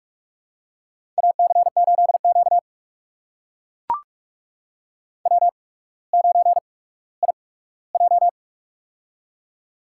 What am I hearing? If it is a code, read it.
Morse code, heard as a keyed tone, about 44 wpm, spaced as for 19 wpm, 690 Hz. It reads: AC8Q W9IJ
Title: USA Call Signs - 4 Characters - 8x Character Spacing - 45wpm